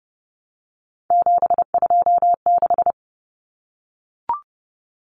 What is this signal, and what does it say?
Morse code, heard as a keyed tone, about 30 wpm, 700 Hz. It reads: 726
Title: Common Cut Numbers - 3-Digit 30wpm